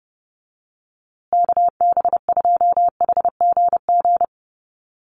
Morse code, heard as a keyed tone, about 30 wpm, 700 Hz. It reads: KB2HGG
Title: USA Call Signs Repeated 2x 30wpm